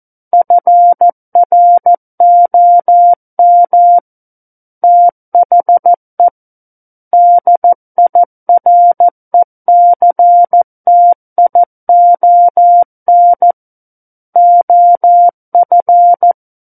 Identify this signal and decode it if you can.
Morse code, heard as a keyed tone, about 14 wpm, 700 Hz. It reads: FROM THE DIRECTION OF
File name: Book One / 05-Chapter 5 - 14 WPM